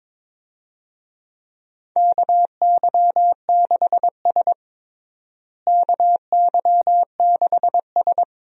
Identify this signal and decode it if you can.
Morse code, heard as a keyed tone, about 22 wpm, 700 Hz. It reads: KY6S KY6S
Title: USA Call Signs - 4 Characters - Repeated 2x - 22wpm